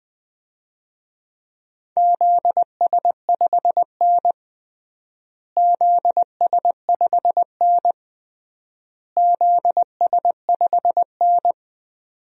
Morse code, heard as a keyed tone, about 20 wpm, 700 Hz. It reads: ZS5N ZS5N ZS5N
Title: DXCC-100 Call Signs - 4 Characters - Repeated 3x - 20wpm